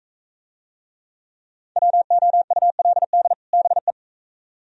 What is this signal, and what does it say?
Morse code, heard as a keyed tone, about 36 wpm, 700 Hz. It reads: WOULD BE